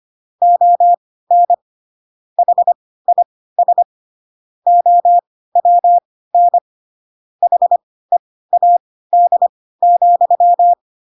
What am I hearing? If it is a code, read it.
Morse code, heard as a keyed tone, about 25 wpm, spaced as for 16 wpm, 700 Hz. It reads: ON HIS OWN HEAD,
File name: Pinocchio / Pinocchio ch3 10 25 ik1hge